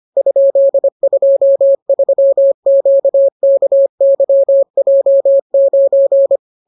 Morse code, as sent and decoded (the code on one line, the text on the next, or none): ..--.. ..--- ...-- --.- -.- -.-- .--- ----.
?23QKYJ9